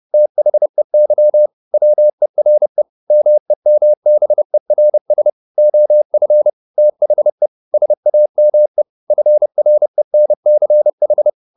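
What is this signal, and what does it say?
Morse code, heard as a keyed tone, about 30 wpm, 600 Hz. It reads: THEY WERE MEMBERS OF THE SAME FRENCH
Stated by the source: Morse code at 30 wpm, the novel Never by Ken Follett in English, from CW med Gnister.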